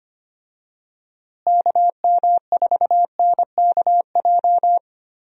Morse code, heard as a keyed tone, about 25 wpm, 700 Hz. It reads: KM4NKJ
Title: USA Call Signs - 6 Characters - Repeated 2x - 25wpm